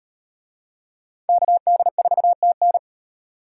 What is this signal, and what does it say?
Morse code, heard as a keyed tone, about 38 wpm, 700 Hz. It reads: KD4TN